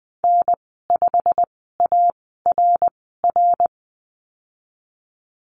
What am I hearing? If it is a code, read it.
Morse code, heard as a keyed tone, about 19 wpm, spaced as for 14 wpm, 700 Hz. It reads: N5ARR
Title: 088 Lesson 15g - Number 5 - Call Signs 20wpm